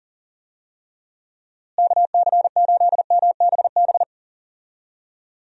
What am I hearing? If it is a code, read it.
Morse code, heard as a keyed tone, about 40 wpm, 700 Hz. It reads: KC8MBB
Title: USA Call Signs - 6 Characters - Repeated 2x - 40wpm